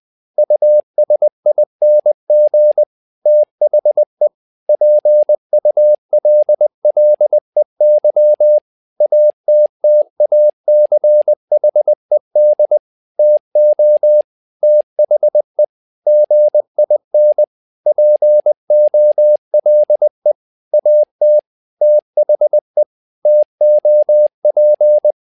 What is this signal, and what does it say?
Morse code, heard as a keyed tone, about 20 wpm, 600 Hz. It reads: USING THE PULLEY ATTACHED TO THE GIN POLE AT THE TOP
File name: tower trouble 20wpm